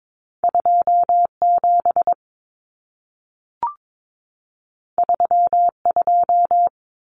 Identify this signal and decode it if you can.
Morse code, heard as a keyed tone, about 22 wpm, 700 Hz. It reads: 27 32
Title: Numbers - 2-Digit 22wpm